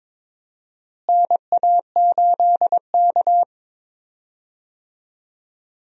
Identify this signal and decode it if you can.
Morse code, heard as a keyed tone, about 23 wpm, 700 Hz. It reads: NA8K
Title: USA Call Signs - 4 Characters - Repeated 3x - 22wpm